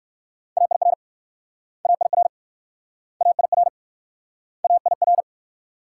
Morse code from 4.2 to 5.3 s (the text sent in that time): AIR